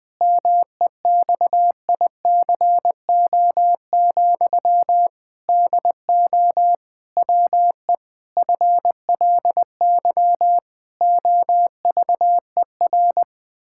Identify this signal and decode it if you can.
Morse code, heard as a keyed tone, about 21 wpm, 700 Hz. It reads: MEXICO, DO WE FLY OVER
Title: Pangram Challenge v2 - 20wpm